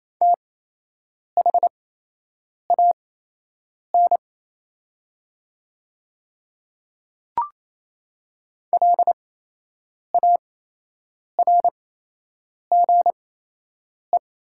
Morse code, heard as a keyed tone, about 28 wpm, spaced as for 8 wpm, 700 Hz. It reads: THAN LARGE
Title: 8x Character Spacing - Top 300 Words - 28wpm